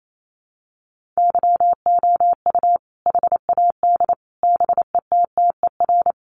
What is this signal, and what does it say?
Morse code, heard as a keyed tone, about 28 wpm, 700 Hz. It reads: YOU HAD BETTER
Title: Sentences from Top 400 Words 28wpm